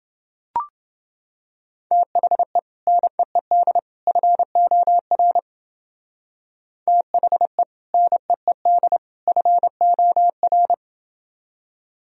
Morse code, heard as a keyed tone, about 30 wpm, 700 Hz. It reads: THE NEED FOR THE NEED FOR